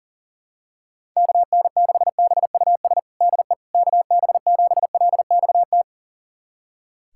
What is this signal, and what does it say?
Morse code, heard as a keyed tone, about 40 wpm, 690 Hz. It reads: KN6BUS DE KB7LXT
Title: USA Callsign DE Callsign - 6 Characters - 40wpm